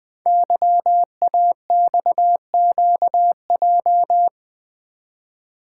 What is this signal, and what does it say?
Morse code, heard as a keyed tone, about 20 wpm, 700 Hz. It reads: YAXQJ
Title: Code Groups - 5 Letters - 20wpm